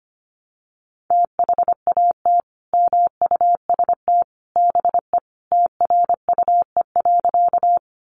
Morse code, heard as a keyed tone, about 25 wpm, 700 Hz. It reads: THAT MUST BE TRUE.
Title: Sentences from Top 400 Words 25wpm